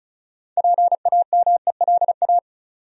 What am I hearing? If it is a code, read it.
Morse code, heard as a keyed tone, about 36 wpm, 700 Hz. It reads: PAMELA
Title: USA Common Names - 35wpm